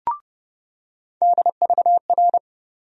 Morse code, heard as a keyed tone, about 32 wpm, 700 Hz. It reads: DVR
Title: USA Call Sign Postfixes - 3 Letters 30wpm